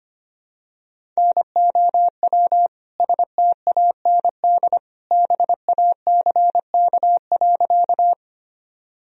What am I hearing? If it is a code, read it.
Morse code, heard as a keyed tone, about 25 wpm, 700 Hz. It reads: NOW STAND BACK.
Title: Sentences from Top 1000 Words - Review Part 1 - 25wpm